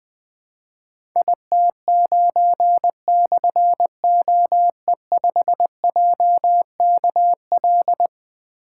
Morse code, heard as a keyed tone, about 20 wpm, 700 Hz. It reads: IT9/OE5JKL
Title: Difficult Call Signs 20wpm